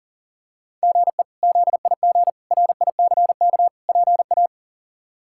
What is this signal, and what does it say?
Morse code, heard as a keyed tone, about 40 wpm, 700 Hz. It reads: GE ZIG RICK PA
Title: K1USN SST QSO Practice - 40wpm